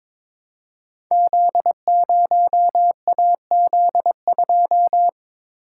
Morse code, heard as a keyed tone, about 22 wpm, 700 Hz. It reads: Z0AZ2
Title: Code Groups - 5 Letter:Numbers - 22wpm